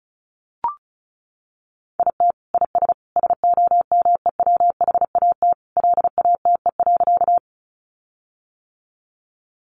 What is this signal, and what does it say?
Morse code, heard as a keyed tone, about 34 wpm, 700 Hz. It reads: IT IS SOMEWHAT LATE.
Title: Sentences from Top 1300 Words 35wpm